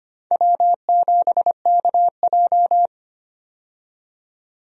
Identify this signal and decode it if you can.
Morse code, heard as a keyed tone, about 25 wpm, 700 Hz. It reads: W7KJ
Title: USA Call Signs - 4 Characters - 25wpm